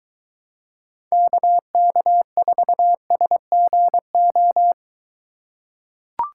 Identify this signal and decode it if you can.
Morse code, heard as a keyed tone, about 23 wpm, 700 Hz. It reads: KK4SGO